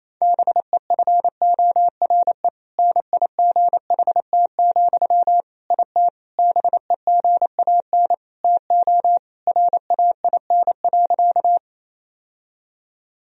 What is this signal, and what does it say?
Morse code, heard as a keyed tone, about 28 wpm, 700 Hz. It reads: BEFORE NIGHT, IT BEGAN TO RAIN.
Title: Sentences from Top 1000 Words - Review Part 1 - 28wpm